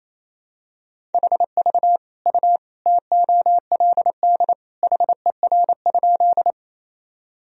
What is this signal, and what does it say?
Morse code, heard as a keyed tone, about 28 wpm, 700 Hz. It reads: HV U TOLD HER?
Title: Sentences from Top 400 Words and Abbreviations 28wpm